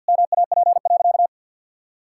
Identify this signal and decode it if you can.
Morse code, heard as a keyed tone, about 50 wpm, 700 Hz. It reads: MAP.
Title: Sentences from Top 1000 Words - Review Part 6 - 50wpm